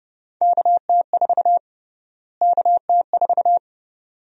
Morse code, heard as a keyed tone, about 30 wpm, 700 Hz. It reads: KT4 KT4
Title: USA Call Sign Prefixes - Repeated 2x - 30wpm